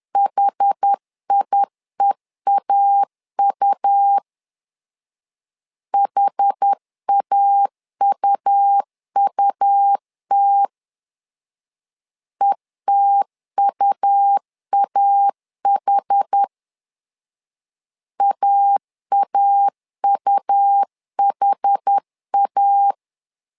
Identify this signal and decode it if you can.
Morse code, heard as a keyed tone, about 10 wpm, 800 Hz. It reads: HIEAU HAUUT ETUAH AAUHA
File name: LEZIONI / 2 UHDN / 2 UH 10wpm 1714